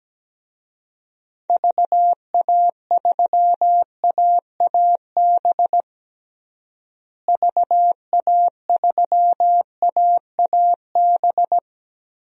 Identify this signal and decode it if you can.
Morse code, heard as a keyed tone, about 17 wpm, 700 Hz. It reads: VA3AAB VA3AAB
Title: DXCC-100 Call Signs - 6 Characters - Repeated 3x - 17wpm